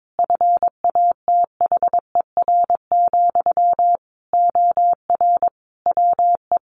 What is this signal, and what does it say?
Morse code, heard as a keyed tone, about 22 wpm, 700 Hz. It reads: FATHER, OR WE
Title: Sentences from Top 1300 Words 22wpm